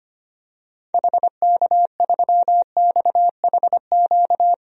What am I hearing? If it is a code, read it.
Morse code, heard as a keyed tone, about 25 wpm, 700 Hz. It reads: HK3XHQ
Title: DXCC-100 Call Signs - 6 Characters - 25wpm